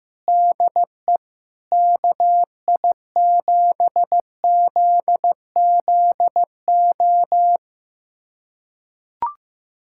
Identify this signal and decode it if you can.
Morse code, heard as a keyed tone, about 15 wpm, 700 Hz. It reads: DE KI7ZZO
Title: USA Callsign DE Callsign - 6 Characters - 15wpm